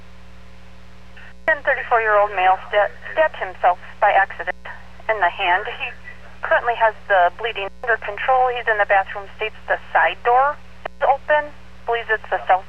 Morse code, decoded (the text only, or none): none